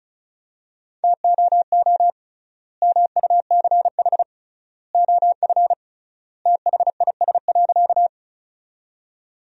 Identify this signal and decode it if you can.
Morse code, heard as a keyed tone, about 34 wpm, 700 Hz. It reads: TOO MUCH OF THIS.
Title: 3x Word-Spacing - Sentences from Top 200 Words - 35wpm